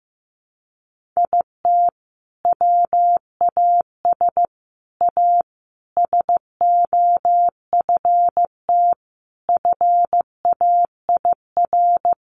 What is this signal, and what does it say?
Morse code, heard as a keyed tone, about 15 wpm, 700 Hz. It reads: IT WAS A SOFT FAIR
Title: Sentences from Top 1500 Words 15wpm